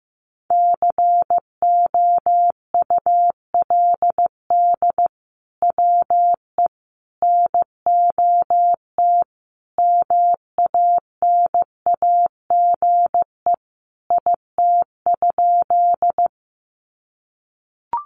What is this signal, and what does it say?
Morse code, heard as a keyed tone, about 15 wpm, 700 Hz. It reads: COULD WE NOT MANAGE IT?
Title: Sentences from Top 1400 Words 15wpm